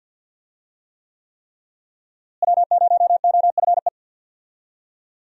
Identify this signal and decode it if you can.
Morse code, heard as a keyed tone, about 47 wpm, 700 Hz. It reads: W0OFE
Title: USA Call Signs - 5 Characters - Repeated 3x - 50wpm